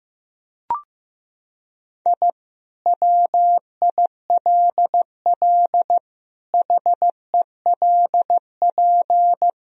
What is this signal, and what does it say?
Morse code, heard as a keyed tone, about 16 wpm, 700 Hz. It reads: I WILL HELP
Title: Sentences from Top 1000 Words - Review Part 3 - 15wpm